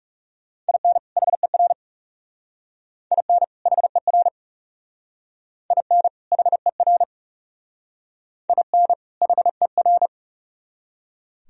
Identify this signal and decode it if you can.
Morse code, heard as a keyed tone, about 40 wpm, 700 Hz. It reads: IN HER IN HER IN HER IN HER